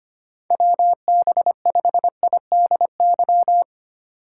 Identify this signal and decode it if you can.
Morse code, heard as a keyed tone, about 25 wpm, 700 Hz. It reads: WB5IDY
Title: K1USN SST QSO Practice - 25wpm